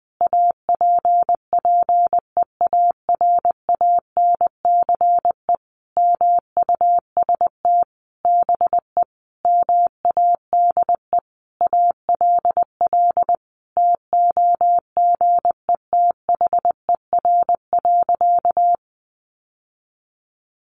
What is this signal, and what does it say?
Morse code, heard as a keyed tone, about 19 wpm, 700 Hz. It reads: APPEARANCE MUST BE MADE ALL TOGETHER.
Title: Sentences from Top 1100 Words 20wpm